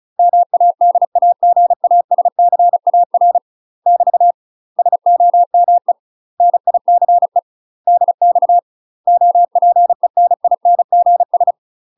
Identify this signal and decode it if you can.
Morse code, heard as a keyed tone, about 34 wpm, 700 Hz. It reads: MADAGASCAR <BT> SOME NICE DX OPENINGS
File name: UK MASTER QSOs TRAINING / 35WPM / UK MASTER QSOs 0043